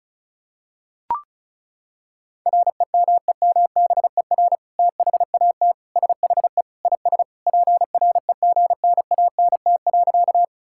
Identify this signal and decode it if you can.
Morse code, heard as a keyed tone, about 34 wpm, 700 Hz. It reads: REMEMBER THAT SHE IS PREGNANT.
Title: Sentences from Top 1000 Words - Review Part 6 - 35wpm